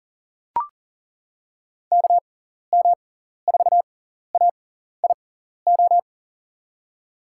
Morse code, heard as a keyed tone, about 40 wpm, spaced as for 14 wpm, 700 Hz. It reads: KM4AIO
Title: USA Call Signs - 6x Character Spacing - 40wpm